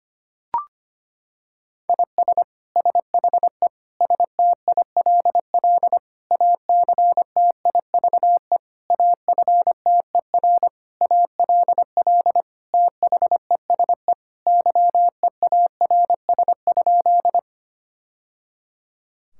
Morse code, heard as a keyed tone, about 25 wpm, 700 Hz. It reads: IS SHE STILL ACTIVE AFTER ALL THESE YEARS?
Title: Sentences from Top 1000 Words - Review Part 6 - 25wpm